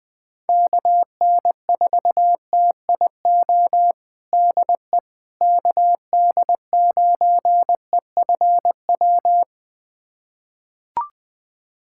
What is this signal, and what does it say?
Morse code, heard as a keyed tone, about 21 wpm, 700 Hz. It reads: KN4TIO DE KD9EFW
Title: USA Callsign DE Callsign - 6 Characters - 20wpm